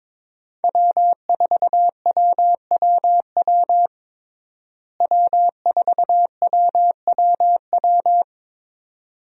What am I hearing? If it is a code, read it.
Morse code, heard as a keyed tone, about 22 wpm, 700 Hz. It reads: W4WWW W4WWW
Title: USA Call Signs - 5 Characters - Repeated 2x - 22wpm